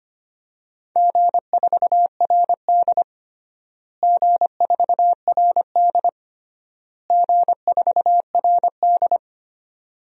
Morse code, heard as a keyed tone, about 25 wpm, 700 Hz. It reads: G4RD G4RD G4RD
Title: DXCC-100 Call Signs - 4 Characters - Repeated 3x - 25wpm